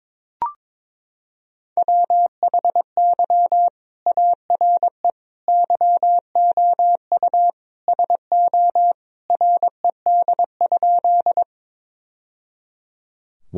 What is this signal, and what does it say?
Morse code, heard as a keyed tone, about 21 wpm, 700 Hz. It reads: WHY ARE YOU SO RED?
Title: Sentences from Top 1000 Words - Review Part 2 - 22wpm